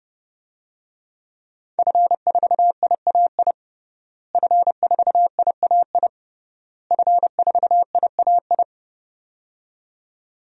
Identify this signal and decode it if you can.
Morse code, heard as a keyed tone, about 29 wpm, 700 Hz. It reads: F4IAI F4IAI F4IAI
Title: DXCC-100 Call Signs - 5 Characters - Repeated 3x - 30wpm